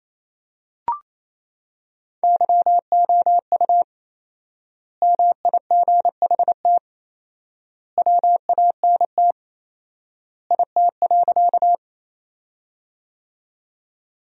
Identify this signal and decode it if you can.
Morse code, heard as a keyed tone, about 28 wpm, 700 Hz. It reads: YOU MIGHT WANT IT.
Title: 4x Word-Spacing - Sentences from Top 300 Words - 28wpm